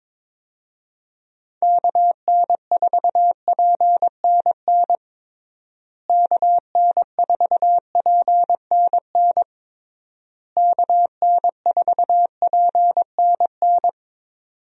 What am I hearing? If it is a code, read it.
Morse code, heard as a keyed tone, about 22 wpm, 700 Hz. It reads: KN4PNN KN4PNN KN4PNN
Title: USA Call Signs - 6 Characters - Repeated 3x - 22wpm